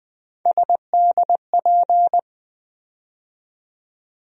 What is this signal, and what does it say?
Morse code, heard as a keyed tone, about 20 wpm, 700 Hz. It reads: SDP
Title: USA Call Sign Postfixes - 3 Letters 20wpm